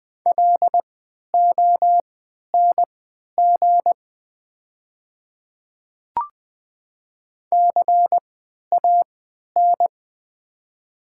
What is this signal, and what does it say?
Morse code, heard as a keyed tone, about 20 wpm, spaced as for 11 wpm, 700 Hz. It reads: LONG CAN
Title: 3x Character Spacing - Top 100 Words - 20wpm